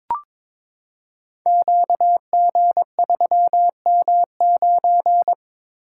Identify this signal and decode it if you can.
Morse code, heard as a keyed tone, about 22 wpm, 700 Hz. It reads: QG3M9
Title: Code Groups - 5 Letter:Numbers - 22wpm